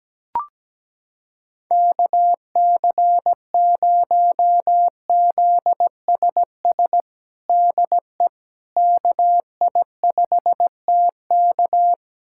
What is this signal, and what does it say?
Morse code, heard as a keyed tone, about 17 wpm, 700 Hz. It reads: KC0ZSS DE KI5TK
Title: USA Callsign DE Callsign 17wpm